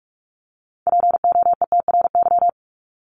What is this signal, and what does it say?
Morse code, heard as a keyed tone, about 44 wpm, 700 Hz. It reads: POETRY
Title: Top 1751-2000 Words 45wpm